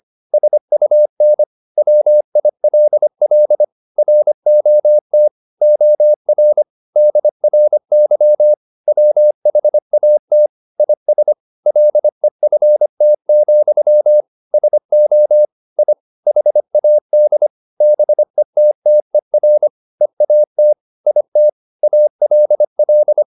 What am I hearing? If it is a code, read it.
Morse code, heard as a keyed tone, about 25 wpm, 600 Hz. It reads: SUN WILL ROT OR DRY WHAT IS LEFT, SO I HAD BETTER EAT IT ALL
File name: the old man and the sea part 1 25wpm